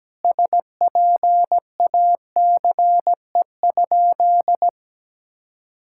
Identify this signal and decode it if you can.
Morse code, heard as a keyed tone, about 17 wpm, 700 Hz. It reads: SPACE?
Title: Sentences from Top 1000 Words - Review Part 5 - 17wpm